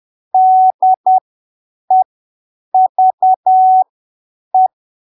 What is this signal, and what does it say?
Morse code, heard as a keyed tone, about 10 wpm, spaced as for 7 wpm, 750 Hz. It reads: DEVE
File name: arquivos / texto / texto 20 10ppm